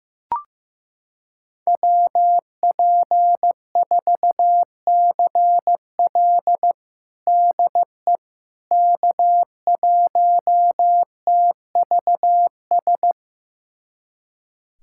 Morse code, heard as a keyed tone, about 15 wpm, 700 Hz. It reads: WP4CL DE K1TVS
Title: USA Callsign DE Callsign - 5 Characters - 15wpm